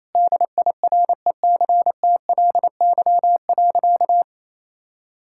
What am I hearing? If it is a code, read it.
Morse code, heard as a keyed tone, about 28 wpm, 700 Hz. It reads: DIRECTLY.
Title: Sentences from Top 1000 Words - Review Part 5 - 28wpm